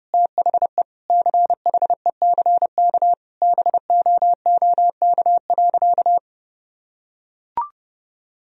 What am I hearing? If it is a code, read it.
Morse code, heard as a keyed tone, about 32 wpm, 700 Hz. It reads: THE CHECK BOOK.
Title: Sentences from Top 1000 Words - Review Part 3 - 30wpm